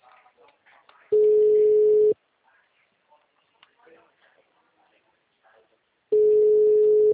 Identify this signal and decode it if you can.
Ringback tone (Europe, 425 Hz), with speech.